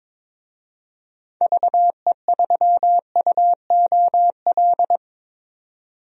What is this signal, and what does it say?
Morse code, heard as a keyed tone, about 22 wpm, 700 Hz. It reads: VE3UOL
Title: DXCC-100 Call Signs - 6 Characters - 22wpm